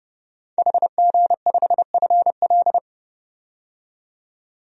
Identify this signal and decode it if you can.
Morse code, heard as a keyed tone, about 30 wpm, 700 Hz. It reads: HG5FL
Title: DXCC-100 Call Signs - 5 Characters - 30wpm